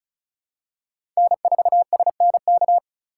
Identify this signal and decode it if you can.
Morse code, heard as a keyed tone, about 34 wpm, 700 Hz. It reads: N4SNK
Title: USA Call Signs - 5 Characters - 3x Character Spacing - 35wpm